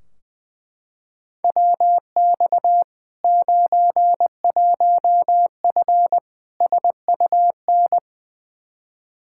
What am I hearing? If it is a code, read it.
Morse code, heard as a keyed tone, about 20 wpm, 700 Hz. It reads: WX 91F SUN
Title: Practice QSO - Exchange 2 - Info - 20wpm